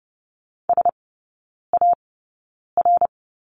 Morse code, heard as a keyed tone, about 30 wpm, spaced as for 9 wpm, 700 Hz. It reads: SAR
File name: Lesson 16e - Letter c - Three Characters 30wpm